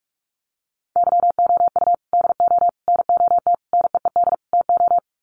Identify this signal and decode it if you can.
Morse code, heard as a keyed tone, about 44 wpm, 700 Hz. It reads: YOU DO NOT NEED TO
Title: Sets of 5 Words 45wpm